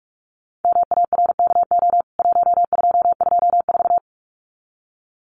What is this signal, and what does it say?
Morse code, heard as a keyed tone, about 44 wpm, 700 Hz. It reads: MARKO 1224
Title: CWT Exchange 45wpm